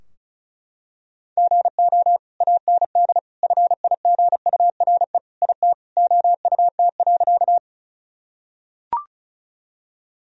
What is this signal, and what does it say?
Morse code, heard as a keyed tone, about 36 wpm, 700 Hz. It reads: GO AND FIGURE IT OUT.
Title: Sentences from Top 1000 Words - Review Part 2 - 35wpm